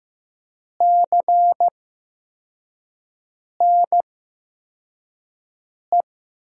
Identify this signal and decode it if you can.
Morse code, heard as a keyed tone, about 15 wpm, spaced as for 4 wpm, 700 Hz. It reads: CNE
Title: USA Call Signs - 6 Characters - 8x Character Spacing - 15wpm